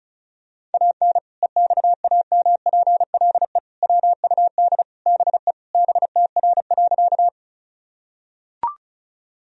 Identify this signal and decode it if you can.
Morse code, heard as a keyed tone, about 34 wpm, 700 Hz. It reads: AN EXAMPLE WUD BE BTR.
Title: Sentences from Top 400 Words and Abbreviations 35wpm